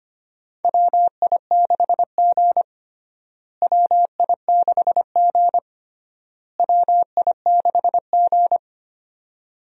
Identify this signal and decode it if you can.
Morse code, heard as a keyed tone, about 25 wpm, 700 Hz. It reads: WI6G WI6G WI6G
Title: USA Call Signs - 4 Characters - Repeated 3x - 25wpm